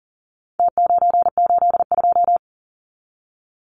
Morse code, heard as a keyed tone, about 40 wpm, 700 Hz. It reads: T982
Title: Numbers - Contest Style - 4-Digit 40wpm